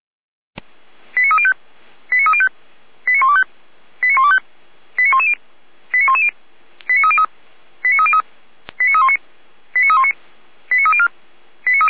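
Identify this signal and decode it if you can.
Five-tone selective call (ZVEI), heard as quick runs of five tones, tones of about 70 ms. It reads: 89305 89305 89125 89125 89110 89110 89303 89303 89319 89319 89304
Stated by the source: ZVEI five-tone selective call 89097 received by SDR.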